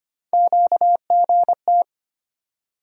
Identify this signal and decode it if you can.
Morse code, heard as a keyed tone, about 25 wpm, 700 Hz. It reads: QGT
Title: USA Call Sign Postfixes - 3 Letters 25wpm